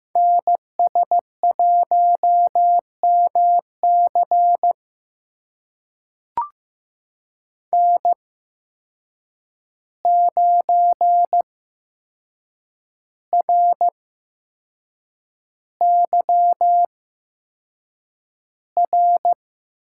Morse code, heard as a keyed tone, about 15 wpm, spaced as for 6 wpm, 700 Hz. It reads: NS1MC N9RYR
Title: USA Call Signs - 8x Character Spacing - 15wpm